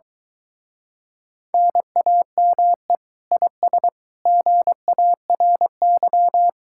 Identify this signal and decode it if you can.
Morse code, heard as a keyed tone, about 23 wpm, 700 Hz. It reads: NAME IS GARY